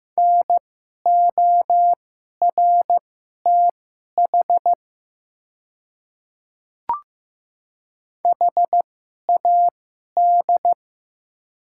Morse code, heard as a keyed tone, about 15 wpm, spaced as for 11 wpm, 700 Hz. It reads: NORTH HAD
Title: 2x Character Spacing - Top 500 Words - 15wpm